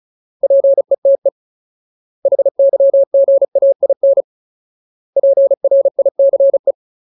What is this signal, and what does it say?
Morse code, heard as a keyed tone, about 34 wpm, 550 Hz. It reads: PETE HYGAIN PRICE
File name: Ultimate Word List / 35WPM Ultimate